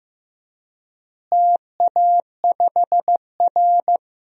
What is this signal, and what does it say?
Morse code, heard as a keyed tone, about 15 wpm, 700 Hz. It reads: TA5R
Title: DXCC-100 Call Signs - 4 Characters - Repeated 3x - 15wpm